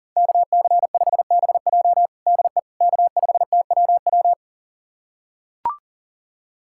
Morse code, heard as a keyed tone, about 40 wpm, 700 Hz. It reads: KC5BJ DE K5TWW
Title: USA Callsign DE Callsign - 5 Characters - 40wpm